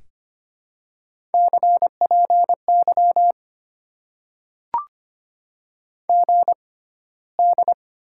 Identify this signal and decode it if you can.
Morse code, heard as a keyed tone, about 25 wpm, spaced as for 15 wpm, 700 Hz. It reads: CPY GD
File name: Top 90 QSO Elements - 6x Character Spacing - 25wpm